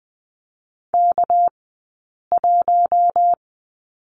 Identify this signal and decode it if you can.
Morse code, heard as a keyed tone, about 20 wpm, spaced as for 8 wpm, 700 Hz. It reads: K1
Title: 197 Lesson 32d - Letter k - Two Characters 20wpm